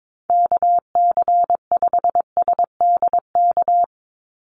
Morse code, heard as a keyed tone, about 22 wpm, 700 Hz. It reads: KC5SDK
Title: USA Call Signs 22wpm